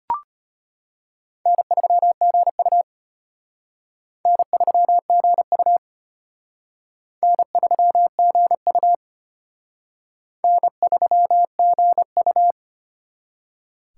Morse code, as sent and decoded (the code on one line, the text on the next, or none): -. ...-- --. ..- / -. ...-- --. ..- / -. ...-- --. ..- / -. ...-- --. ..-
N3GU N3GU N3GU N3GU